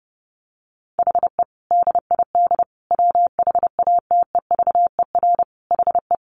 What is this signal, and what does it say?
Morse code, heard as a keyed tone, about 30 wpm, 700 Hz. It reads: HE DID WHATEVER HE
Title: Sentences from Top 1100 Words 30wpm